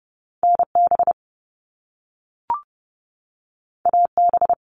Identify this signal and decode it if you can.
Morse code, heard as a keyed tone, about 30 wpm, 700 Hz. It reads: NB AB
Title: Canadian Provinces and Territories 30wpm